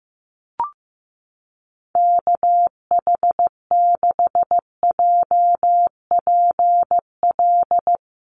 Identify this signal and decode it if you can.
Morse code, heard as a keyed tone, about 15 wpm, 700 Hz. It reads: KH6JPL
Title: USA Call Signs Repeated 3x 15wpm